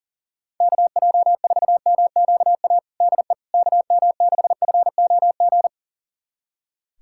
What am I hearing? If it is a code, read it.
Morse code, heard as a keyed tone, about 40 wpm, 690 Hz. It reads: KJ4MQA DE KM6FOG
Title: USA Callsign DE Callsign - 6 Characters - 40wpm